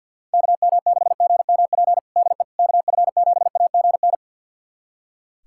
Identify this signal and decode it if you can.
Morse code, heard as a keyed tone, about 46 wpm, 700 Hz. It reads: KM6GMP DE KF7AGN
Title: USA Callsign DE Callsign - 6 Characters - 50wpm